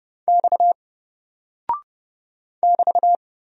Morse code, heard as a keyed tone, about 30 wpm, 700 Hz. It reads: X <BT>
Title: Single Character (Letter:Number:Punctuation:Prosign) 30wpm